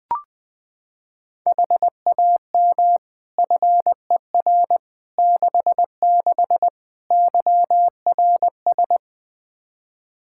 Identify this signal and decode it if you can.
Morse code, heard as a keyed tone, about 20 wpm, 700 Hz. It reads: HAM FER 66 YRS
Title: Practice QSO - Exchange 3 - More Info - 20wpm